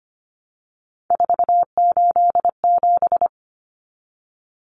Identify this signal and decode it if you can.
Morse code, heard as a keyed tone, about 25 wpm, 700 Hz. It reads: 487
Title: Numbers - 3-Digit 25wpm